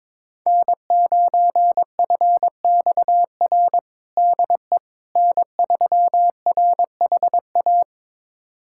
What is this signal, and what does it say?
Morse code, heard as a keyed tone, about 22 wpm, 700 Hz. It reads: N9FXR DE N3RHA
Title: USA Callsign DE Callsign - 5 Characters - 22wpm